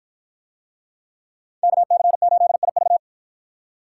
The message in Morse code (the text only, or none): KC8IV